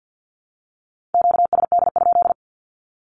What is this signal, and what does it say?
Morse code, heard as a keyed tone, about 48 wpm, 700 Hz. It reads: QSD?
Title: Q-Codes 50wpm